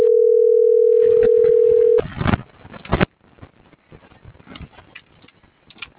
Ringback tone (North America), 440 + 480 Hz.